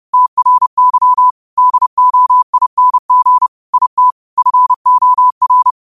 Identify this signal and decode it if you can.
Morse code, heard as a keyed tone, about 30 wpm, 1000 Hz. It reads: TRY DOING IT FOR
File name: qotdfiles / 30wpm